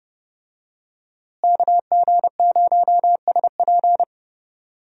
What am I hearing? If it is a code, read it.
Morse code, heard as a keyed tone, about 30 wpm, 700 Hz. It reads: KG0SP
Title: USA Call Signs - 5 Characters - Repeated 3x - 30wpm